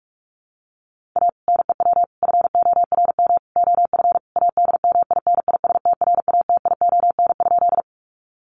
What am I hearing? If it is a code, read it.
Morse code, heard as a keyed tone, about 44 wpm, 700 Hz. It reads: A NEW FORM OF ADMINISTRATION?
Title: Sentences from Top 1300 Words 45wpm